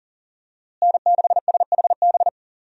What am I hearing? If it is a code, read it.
Morse code, heard as a keyed tone, about 40 wpm, 700 Hz. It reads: N6SHB
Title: USA Call Signs - 5 Characters - Repeated 2x - 40wpm